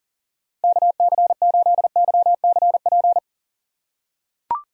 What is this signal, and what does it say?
Morse code, heard as a keyed tone, about 38 wpm, 700 Hz. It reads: KC8YCP
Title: USA Call Signs - 6 Characters - 6x Character Spacing - 40wpm